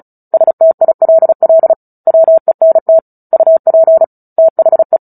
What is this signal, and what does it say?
Morse code, heard as a keyed tone, about 34 wpm, 650 Hz. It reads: STILL WENT UP THE
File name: Book Two / 09-Chapter 9 - 35 WPM